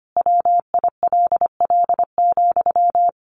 Morse code, heard as a keyed tone, about 25 wpm, 700 Hz. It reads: WILL,
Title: Sentences from Top 1400 Words 25wpm